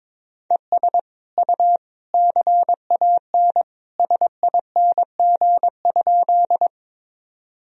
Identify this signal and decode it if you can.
Morse code, heard as a keyed tone, about 21 wpm, 700 Hz. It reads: ES U CAN SING?
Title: Sentences from Top 400 Words and Abbreviations 22wpm